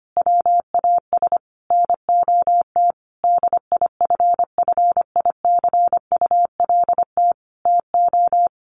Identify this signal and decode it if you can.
Morse code, heard as a keyed tone, about 25 wpm, 700 Hz. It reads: WAS NOT DIFFICULT TO
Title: Sentences from Top 1100 Words 25wpm